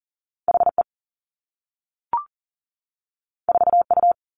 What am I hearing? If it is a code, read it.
Morse code, heard as a keyed tone, about 41 wpm, 700 Hz. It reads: HE 4U
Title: Two Challenging Letter-Number Combinations 40wpm